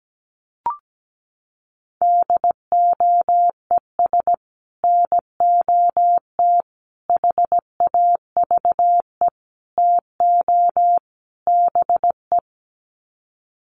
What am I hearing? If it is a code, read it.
Morse code, heard as a keyed tone, about 17 wpm, 700 Hz. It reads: DOES NOT HAVE TO BE
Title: Sets of 5 Words 17wpm